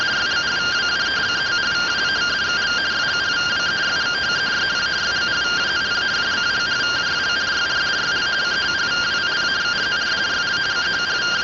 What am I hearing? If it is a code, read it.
Two-tone teleprinter signal, about 170 Hz shift.